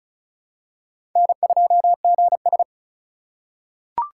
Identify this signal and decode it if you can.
Morse code, heard as a keyed tone, about 36 wpm, 700 Hz. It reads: N2GS
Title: USA Call Signs - 4 Characters - 6x Character Spacing - 35wpm